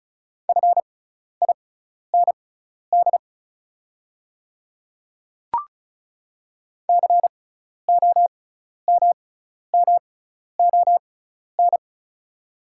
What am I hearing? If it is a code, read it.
Morse code, heard as a keyed tone, about 36 wpm, spaced as for 12 wpm, 700 Hz. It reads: FIND COMMON
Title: 6x Character Spacing - Top 500 Words - 35wpm